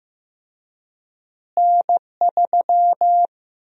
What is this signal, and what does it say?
Morse code, heard as a keyed tone, about 16 wpm, 700 Hz. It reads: N3
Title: USA Call Sign Prefixes - Repeated 2x - 15wpm